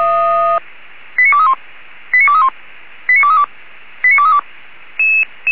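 Five-tone selective call (ZVEI), heard as quick runs of five tones, tones of about 70 ms. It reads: 89231 89231 89232 89232
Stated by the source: ZVEI five-tone call 89244 received by SDR.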